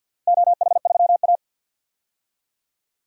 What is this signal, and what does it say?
Morse code, heard as a keyed tone, about 51 wpm, 700 Hz. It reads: OH3A
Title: DXCC-100 Call Signs - 4 Characters - 50wpm